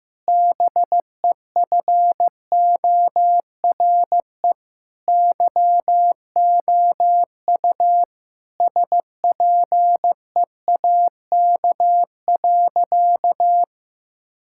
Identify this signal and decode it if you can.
Morse code, heard as a keyed tone, about 15 wpm, 700 Hz. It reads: BEFORE YOU SPEAK.
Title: Sentences from Top 1600 Words 15wpm